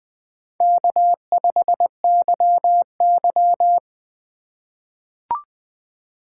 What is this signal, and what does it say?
Morse code, heard as a keyed tone, about 21 wpm, 700 Hz. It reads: K5YY
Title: USA Call Signs - 4 Characters - 6x Character Spacing - 20wpm